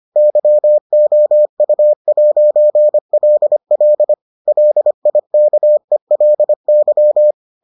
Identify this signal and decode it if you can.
Morse code, heard as a keyed tone, about 25 wpm, 600 Hz. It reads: YOU'LL LIKELY